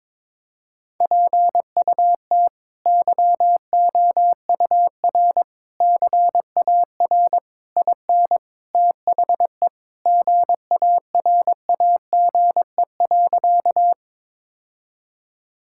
Morse code, heard as a keyed tone, about 22 wpm, 700 Hz. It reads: PUT YOUR CAR IN THE GARAGE.
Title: Sentences from Top 1600 Words 22wpm